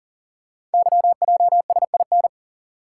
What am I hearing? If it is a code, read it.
Morse code, heard as a keyed tone, about 41 wpm, 700 Hz. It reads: YJSIN
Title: Code Groups - 5 Letters - 40wpm